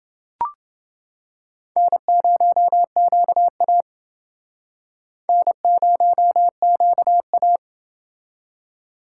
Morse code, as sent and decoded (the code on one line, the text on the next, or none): -. ----- --.- .- / -. ----- --.- .-
N0QA N0QA